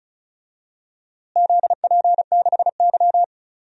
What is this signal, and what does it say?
Morse code, heard as a keyed tone, about 36 wpm, 700 Hz. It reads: ZP6Y
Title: DXCC-100 Call Signs - 4 Characters - Repeated 2x - 35wpm